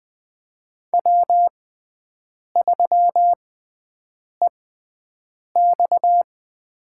Morse code, heard as a keyed tone, about 20 wpm, spaced as for 7 wpm, 700 Hz. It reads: W3EX
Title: USA Call Signs - 4 Characters - 6x Character Spacing - 20wpm